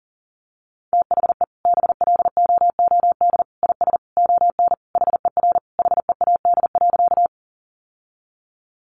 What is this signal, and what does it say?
Morse code, heard as a keyed tone, about 40 wpm, 700 Hz. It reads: THE BLOOD IS ON HER HEAD.
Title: Sentences from Top 1100 Words 40wpm